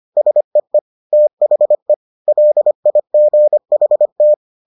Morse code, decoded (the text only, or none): SEE THE LIGHT